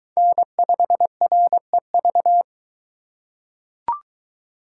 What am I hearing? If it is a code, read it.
Morse code, heard as a keyed tone, about 21 wpm, 700 Hz. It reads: N5REV